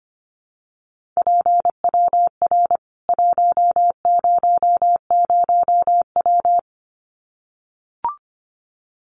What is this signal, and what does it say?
Morse code, heard as a keyed tone, about 25 wpm, 700 Hz. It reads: PWR 100W
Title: Common QSO Phrases 25wpm